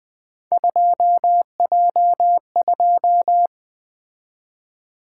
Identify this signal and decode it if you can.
Morse code, heard as a keyed tone, about 20 wpm, 700 Hz. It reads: 2J2
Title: Three Letter-Number Combinations - 20wpm